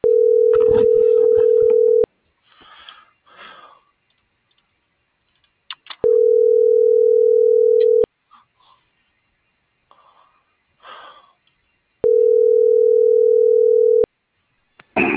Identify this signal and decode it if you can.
Ringback tone (North America), 440 + 480 Hz.